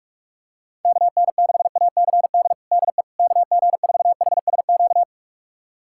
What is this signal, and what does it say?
Morse code, heard as a keyed tone, about 46 wpm, 700 Hz. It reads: KN6ACD DE KG4HSQ
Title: USA Callsign DE Callsign 45wpm